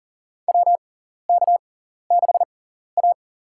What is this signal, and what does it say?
Morse code, heard as a keyed tone, about 39 wpm, spaced as for 14 wpm, 700 Hz. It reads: WK6A
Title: USA Call Signs - 4 Characters - 6x Character Spacing - 40wpm